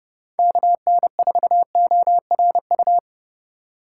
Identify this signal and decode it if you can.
Morse code, heard as a keyed tone, about 31 wpm, 700 Hz. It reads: KN4ORU
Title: USA Call Signs - 6 Characters - 4x Character Spacing - 30wpm